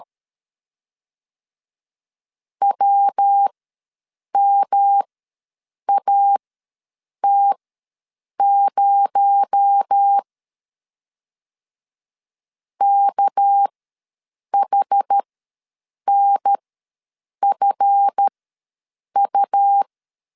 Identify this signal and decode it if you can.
Morse code, heard as a keyed tone, about 13 wpm, spaced as for 7 wpm, 800 Hz. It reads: WMAT0 KHNFU
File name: LEZIONI / 7 GF9Z / 7 GF ws 12wpm 3140